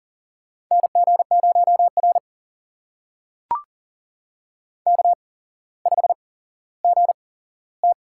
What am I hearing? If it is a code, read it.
Morse code, heard as a keyed tone, about 38 wpm, spaced as for 17 wpm, 700 Hz. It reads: NG0R K5GT
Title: USA Call Signs - 4 Characters - 8x Character Spacing - 40wpm